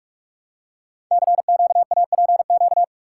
Morse code, heard as a keyed tone, about 44 wpm, 700 Hz. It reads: CQAPQ
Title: Code Groups - 5 Letters - 45wpm